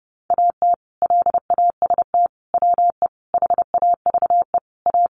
Morse code, heard as a keyed tone, about 30 wpm, 700 Hz. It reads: AT LAST WE HAVE A
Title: Sentences from Top 300 Words 30wpm